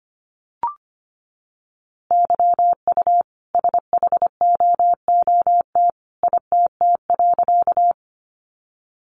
Morse code, heard as a keyed tone, about 25 wpm, 700 Hz. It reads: YU SHOOT ITT.
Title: A Sound Of Thunder 25wpm